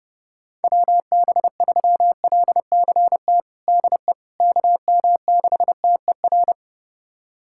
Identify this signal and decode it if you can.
Morse code, heard as a keyed tone, about 32 wpm, 700 Hz. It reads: WB3LCT DE KM6TER
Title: USA Callsign DE Callsign 30wpm